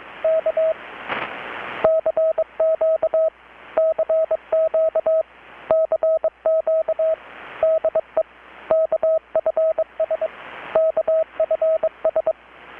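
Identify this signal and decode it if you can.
Morse code, heard as a keyed tone, about 23 wpm, 640 Hz, with radio noise. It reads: K CQ CQ CQ DE KFS KFS
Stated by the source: Morse from KPH's Night of Nights XX maritime radiotelegraphy event, received in Chiapas, Mexico about 3,800 km away on a Kenwood R-600 with a grounded 10 m long-wire antenna.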